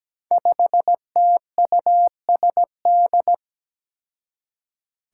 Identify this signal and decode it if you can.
Morse code, heard as a keyed tone, about 17 wpm, 700 Hz. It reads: 5TUSD
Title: Five Challenging Letter-Number Combinations 17wpm